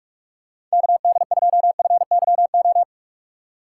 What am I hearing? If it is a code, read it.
Morse code, heard as a keyed tone, about 46 wpm, 700 Hz. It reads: KD2FYO
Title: USA Call Signs - 6 Characters - 4x Character Spacing - 45wpm